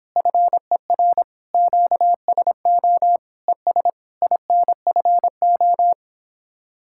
Morse code, heard as a keyed tone, about 29 wpm, 700 Hz. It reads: FER QSO ES INFO